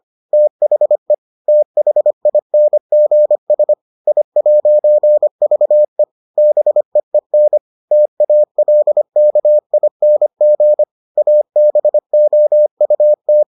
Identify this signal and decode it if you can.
Morse code, heard as a keyed tone, about 25 wpm, 600 Hz. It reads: THE THINGS I'VE BEEN TALKING ABOUT